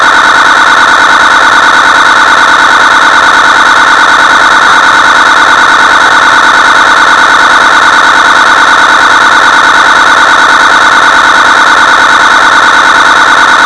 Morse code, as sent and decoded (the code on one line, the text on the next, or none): none